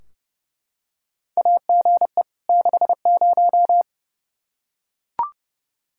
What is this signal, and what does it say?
Morse code, heard as a keyed tone, about 30 wpm, 700 Hz. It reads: AGE 60
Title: Practice QSO - Exchange 3 - More Info - 30wpm